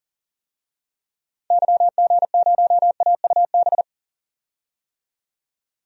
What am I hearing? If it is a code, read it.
Morse code, heard as a keyed tone, about 37 wpm, 700 Hz. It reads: YG0AUB
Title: DXCC-100 Call Signs - 6 Characters - 40wpm